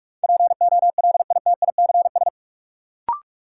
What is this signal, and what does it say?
Morse code, heard as a keyed tone, about 45 wpm, 700 Hz. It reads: POLITICS
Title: Sentences from Top 1000 Words - New Word Drill - 45wpm